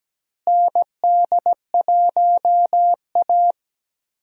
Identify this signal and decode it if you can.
Morse code, heard as a keyed tone, about 17 wpm, 700 Hz. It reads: ND1A
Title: USA Call Signs - 4 Characters - Repeated 3x - 17wpm